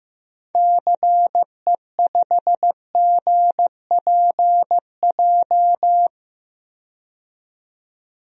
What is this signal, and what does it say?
Morse code, heard as a keyed tone, about 15 wpm, 700 Hz. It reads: CE5GPJ
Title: DXCC-100 Call Signs - 6 Characters - 15wpm